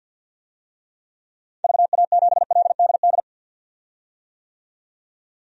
Morse code, heard as a keyed tone, about 47 wpm, 700 Hz. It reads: VA7LDD
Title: DXCC-100 Call Signs - 6 Characters - 50wpm